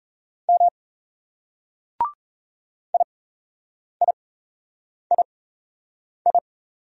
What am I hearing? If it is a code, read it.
Morse code, heard as a keyed tone, about 36 wpm, spaced as for 8 wpm, 700 Hz. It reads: M IIII